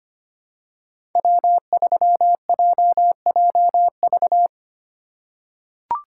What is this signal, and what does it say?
Morse code, heard as a keyed tone, about 25 wpm, 700 Hz. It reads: W3JJV
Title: USA Call Signs - 8x Character Spacing - 25wpm